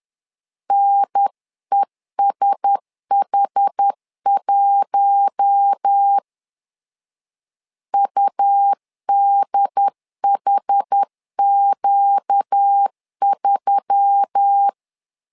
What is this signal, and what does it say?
Morse code, heard as a keyed tone, about 10 wpm, 800 Hz. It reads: NESH1 UDHQ3